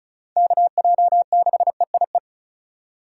Morse code, heard as a keyed tone, about 34 wpm, 690 Hz. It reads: KJ6EIE
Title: USA Call Signs - 6 Characters - 4x Character Spacing - 35wpm